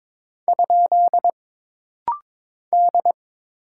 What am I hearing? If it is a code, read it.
Morse code, heard as a keyed tone, about 22 wpm, 700 Hz. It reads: ? D
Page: This rapid-fire timing is geared to push your ICR (Instant Character Recognition) skills.